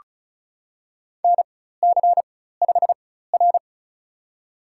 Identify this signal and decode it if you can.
Morse code, heard as a keyed tone, about 34 wpm, spaced as for 16 wpm, 700 Hz. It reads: NC5R
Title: USA Call Signs - 4 Characters - 4x Character Spacing - 35wpm